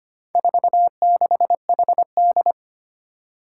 Morse code, heard as a keyed tone, about 25 wpm, 700 Hz. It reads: 46HD
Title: Four Challenging Letter-Number Combinations 25wpm